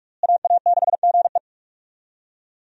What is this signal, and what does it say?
Morse code, heard as a keyed tone, about 46 wpm, 700 Hz. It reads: AA6GE
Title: USA Call Signs - 5 Characters - Repeated 3x - 45wpm